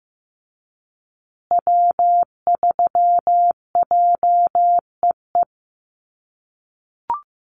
Morse code, heard as a keyed tone, about 15 wpm, 700 Hz. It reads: W3JEE
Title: USA Call Signs Repeated 2x 15wpm